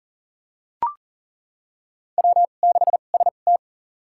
Morse code, heard as a keyed tone, about 40 wpm, spaced as for 29 wpm, 700 Hz. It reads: W6ST
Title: USA Call Signs - 4 Characters - 2x Character Spacing - 40wpm